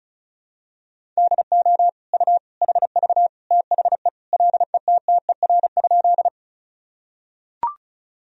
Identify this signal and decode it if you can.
Morse code, heard as a keyed tone, about 35 wpm, 700 Hz. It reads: DO U HV THE LETTER?
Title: Sentences from Top 300 Words and Abbreviations 35wpm